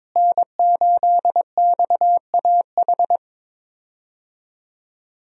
Morse code, heard as a keyed tone, about 22 wpm, 700 Hz. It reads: N8XAH
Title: USA Call Signs - 5 Characters - Repeated 3x - 22wpm